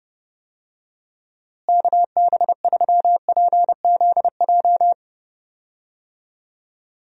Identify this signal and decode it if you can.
Morse code, heard as a keyed tone, about 30 wpm, 700 Hz. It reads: KB3PZJ
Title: USA Call Signs - 6 Characters - Repeated 2x - 30wpm